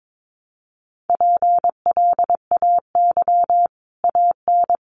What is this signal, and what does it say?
Morse code, heard as a keyed tone, about 22 wpm, 700 Hz. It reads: PLAY AN
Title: Sets of 5 Words Encore 22wpm